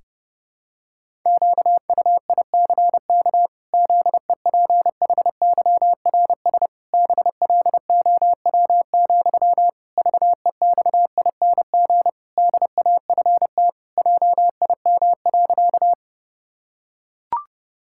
Morse code, heard as a keyed tone, about 30 wpm, 700 Hz. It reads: QUICK ZEPHYRS BLOW, VEXING DAFT JIM.
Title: Pangram Challenge v2 - 30wpm